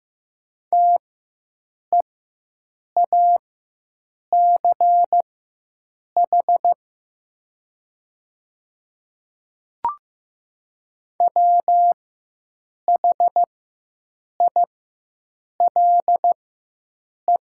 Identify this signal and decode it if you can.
Morse code, heard as a keyed tone, about 15 wpm, spaced as for 7 wpm, 700 Hz. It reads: TEACH WHILE
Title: 4x Character Spacing - Top 500 Words - 15wpm